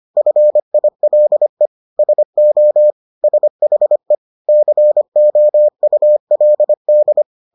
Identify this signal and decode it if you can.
Morse code, heard as a keyed tone, about 25 wpm, 600 Hz. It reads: FILE SO SHE COULD